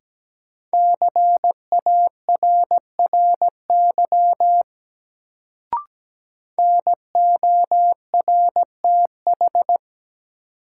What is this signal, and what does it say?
Morse code, heard as a keyed tone, about 17 wpm, 700 Hz. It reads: CARRY NORTH
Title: Sentences from Top 400 Words - New Word Drill - 17wpm